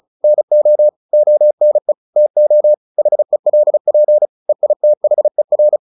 Morse code, heard as a keyed tone, about 34 wpm, 600 Hz. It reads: NO ONE TO HELP EITHER
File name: the old man and the sea / the old man and the sea 1 35wpm